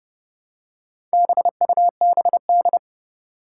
Morse code, heard as a keyed tone, about 30 wpm, 700 Hz. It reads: BUBD
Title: Four Challenging Letter-Number Combinations 30wpm